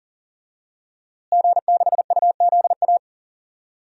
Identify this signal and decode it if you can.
Morse code, heard as a keyed tone, about 39 wpm, 700 Hz. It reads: G6UZA
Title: DXCC-100 Call Signs - 5 Characters - Repeated 2x - 40wpm